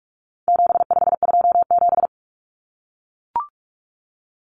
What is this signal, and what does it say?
Morse code, heard as a keyed tone, about 44 wpm, 700 Hz. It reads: 7527
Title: Numbers - 4-Digit 45wpm